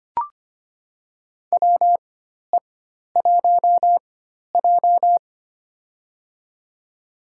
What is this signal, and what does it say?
Morse code, heard as a keyed tone, about 25 wpm, spaced as for 12 wpm, 700 Hz. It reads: WE1J
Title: USA Call Signs - 4 Characters - 4x Character Spacing - 25wpm